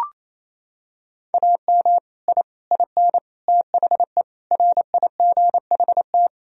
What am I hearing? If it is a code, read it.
Morse code, heard as a keyed tone, about 28 wpm, 700 Hz. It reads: AM I IN THE RIGHT
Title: Sentences from Top 1000 Words - Review Part 2 - 28wpm